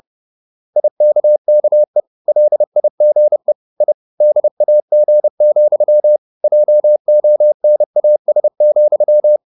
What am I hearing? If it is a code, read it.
Morse code, heard as a keyed tone, about 30 wpm, 600 Hz. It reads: IKKE LIGE I DAG, JONAS,
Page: Ferie 30 wpm